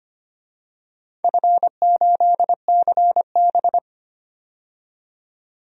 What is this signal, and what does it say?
Morse code, heard as a keyed tone, about 25 wpm, 700 Hz. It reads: F8CB
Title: DXCC-100 Call Signs - 4 Characters - 25wpm